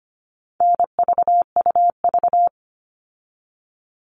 Morse code, heard as a keyed tone, about 25 wpm, 700 Hz. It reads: NVUV
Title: All Cut Numbers - 4-Digit 25wpm